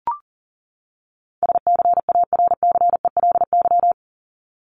Morse code, heard as a keyed tone, about 40 wpm, 700 Hz. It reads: SCARCELY
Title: LY Suffix - Top 100 Words with Base Words 40wpm